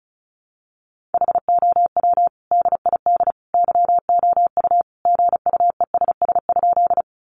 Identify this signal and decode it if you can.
Morse code, heard as a keyed tone, about 34 wpm, 700 Hz. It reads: HOW DID YOU GUESS?
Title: Sentences from Top 1500 Words 35wpm